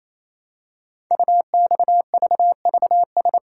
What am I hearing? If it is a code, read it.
Morse code, heard as a keyed tone, about 27 wpm, 700 Hz. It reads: UXVVS
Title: Code Groups - 5 Letters - 28wpm